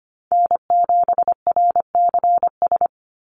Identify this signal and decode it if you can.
Morse code, heard as a keyed tone, about 25 wpm, 700 Hz. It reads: N7RCS
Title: Call Signs Repeated 3x - 25wpm